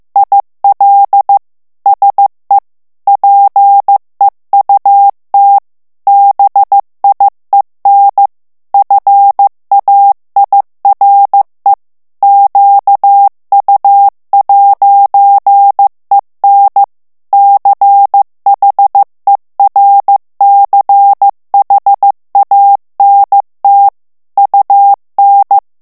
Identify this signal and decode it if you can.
Morse code, heard as a keyed tone, about 15 wpm, 800 Hz. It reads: IL SE PEUT BIEN FAIRE QU'EN CHERCHANT UN
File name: morse mp3 / FRANCAIS TEXTE CLAIR 2 VITESSE 14 MOTS MINUTES